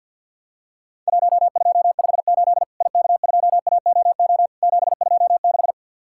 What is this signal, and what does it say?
Morse code, heard as a keyed tone, about 47 wpm, 700 Hz. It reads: 1258 IK2AOO 726
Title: Worked All Europe DX Contest - QTC Exchange - 50wpm